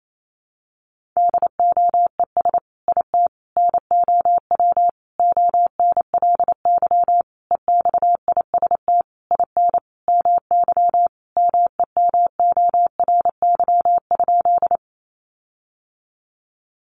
Morse code, heard as a keyed tone, about 28 wpm, 700 Hz. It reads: DOES IT NOW ONLY EXIST IN MY MEMORY?
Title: Sentences from Top 1300 Words 28wpm